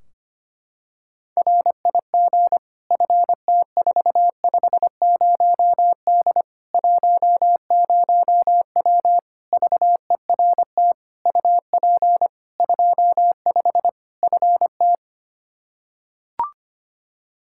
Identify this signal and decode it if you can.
Morse code, heard as a keyed tone, about 25 wpm, 700 Hz. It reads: RIG FT450D 10W VERT UP 25 FT
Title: Practice QSO - Exchange 2 - Info - 25wpm